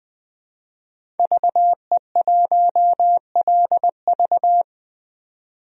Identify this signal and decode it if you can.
Morse code, heard as a keyed tone, about 20 wpm, 700 Hz. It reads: VE1LV
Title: DXCC-100 Call Signs - 5 Characters - 20wpm